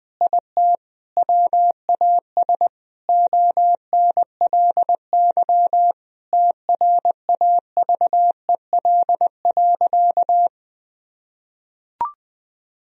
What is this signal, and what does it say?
Morse code, heard as a keyed tone, about 20 wpm, 700 Hz. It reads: IT WAS ONLY TRAVEL.
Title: Sentences from Top 1000 Words - Review Part 2 - 20wpm